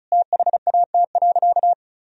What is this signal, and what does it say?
Morse code, heard as a keyed tone, about 34 wpm, 700 Hz. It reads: THAT.
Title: Sentences from Top 200 Words and Abbreviations 35wpm